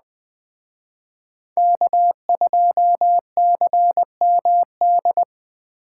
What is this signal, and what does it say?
Morse code, heard as a keyed tone, about 21 wpm, 700 Hz. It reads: K2CMD